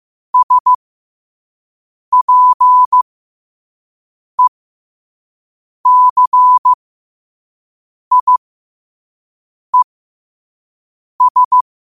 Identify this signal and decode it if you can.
Morse code, heard as a keyed tone, about 15 wpm, spaced as for 5 wpm, 1000 Hz. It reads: SPECIES
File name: qotdfiles / 5wpm